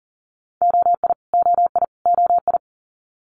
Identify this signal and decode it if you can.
Morse code, heard as a keyed tone, about 40 wpm, 700 Hz. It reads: OI OI OI
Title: Multi-letter Phonemes Repeated 3x 40wpm